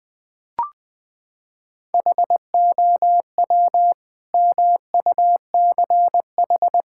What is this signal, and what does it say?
Morse code, heard as a keyed tone, about 20 wpm, 700 Hz. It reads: HOW MUCH
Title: Sentences from Top 1000 Words - Review Part 5 - 20wpm